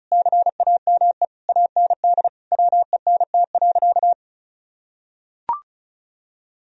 Morse code, heard as a keyed tone, about 34 wpm, 700 Hz. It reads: CAME AND WENT.
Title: Sentences from Top 1000 Words - Review Part 2 - 35wpm